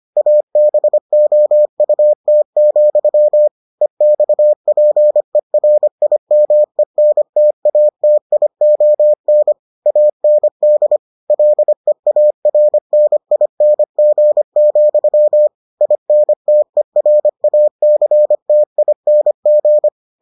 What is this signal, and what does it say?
Morse code, heard as a keyed tone, about 25 wpm, 600 Hz. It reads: ABOUT, EXPERIMENTATION AND LEARNING, INTERACTING